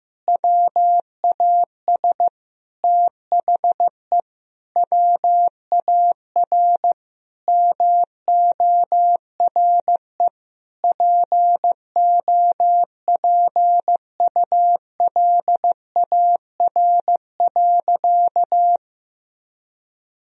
Morse code, heard as a keyed tone, about 15 wpm, 700 Hz. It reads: WAS THE WAR MORE POPULAR.
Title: Sentences from Top 1000 Words - Review Part 3 - 15wpm